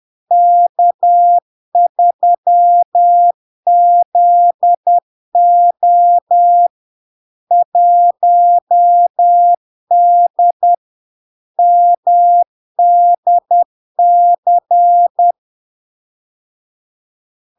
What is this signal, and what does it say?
Morse code, heard as a keyed tone, about 10 wpm, 700 Hz. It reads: K3ZO 1D MDC
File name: LARK-FD-2022-Log-in-CW-10wpm-700-Hz